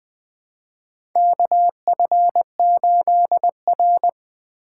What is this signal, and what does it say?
Morse code, heard as a keyed tone, about 20 wpm, 700 Hz. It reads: KF8R
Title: USA Call Signs - 4 Characters - 4x Character Spacing - 20wpm